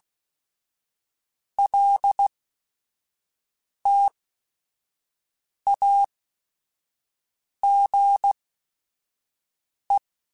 Morse code, heard as a keyed tone, about 16 wpm, spaced as for 5 wpm, 800 Hz. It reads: LTAGE
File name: Technical Subjects / Technical Subjects-mp3s / XC @5